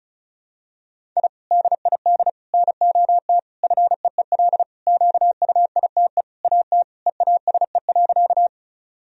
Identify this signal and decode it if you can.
Morse code, heard as a keyed tone, about 34 wpm, 700 Hz. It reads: I DID NOT FEEL QUITE AT EASE.
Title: Sentences from Top 1000 Words - Review Part 1 - 35wpm